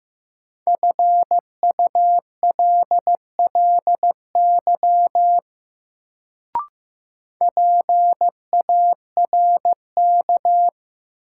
Morse code, heard as a keyed tone, about 15 wpm, 700 Hz. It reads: FULLY PARK
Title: Sentences from Top 1300 Words - New Word Drill Alt - 15wpm